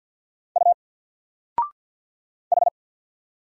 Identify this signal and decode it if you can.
Morse code, heard as a keyed tone, about 50 wpm, 700 Hz. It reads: U H
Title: Single Letter-Number Alt 50wpm